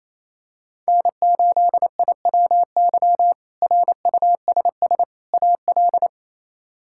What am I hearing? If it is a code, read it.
Morse code, heard as a keyed tone, about 28 wpm, 700 Hz. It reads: N8IWY RUSS AL
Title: K1USN SST QSO Practice - 28wpm